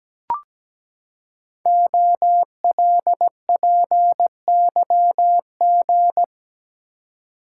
Morse code, heard as a keyed tone, about 17 wpm, 700 Hz. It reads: OLPYG
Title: Code Groups - 5 Letters - 17wpm